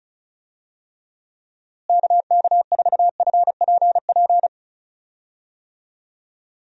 Morse code, heard as a keyed tone, about 36 wpm, 700 Hz. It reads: KK4FPP
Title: USA Call Signs - 6 Characters - Repeated 3x - 35wpm